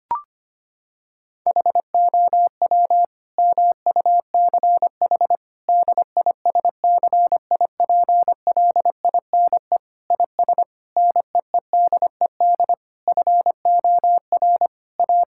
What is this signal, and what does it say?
Morse code, heard as a keyed tone, about 25 wpm, 700 Hz. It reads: HOW MUCH DISCIPLINE IS NEEDED FOR A
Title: Sentences from Top 1000 Words - Review Part 6 - 25wpm